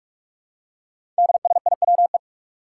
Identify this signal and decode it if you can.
Morse code, heard as a keyed tone, about 44 wpm, 700 Hz. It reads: DSIWE
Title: Five Challenging Letter-Number Combinations 45wpm